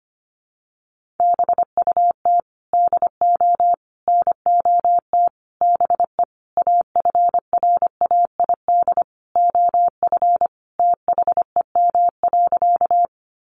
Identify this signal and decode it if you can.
Morse code, heard as a keyed tone, about 25 wpm, 700 Hz. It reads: BUT DO NOT BE AFRAID OF THEM.
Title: Sentences from Top 1500 Words 25wpm